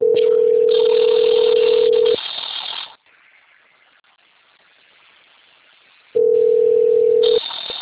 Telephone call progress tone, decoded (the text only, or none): ringback tone